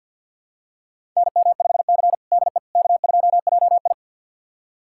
Morse code, heard as a keyed tone, about 52 wpm, 700 Hz. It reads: NM5C DE K2JI
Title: USA Callsign DE Callsign - 4 Characters - 50wpm